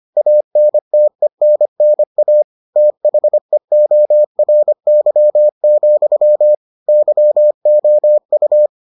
Morse code, heard as a keyed tone, about 25 wpm, 600 Hz. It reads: ANTENNA THEORY, YOU